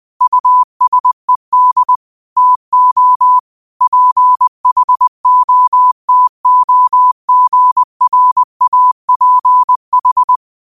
Morse code, heard as a keyed tone, about 20 wpm, 1000 Hz. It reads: USED TO PHOTOGRAPH
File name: qotdfiles / 20wpm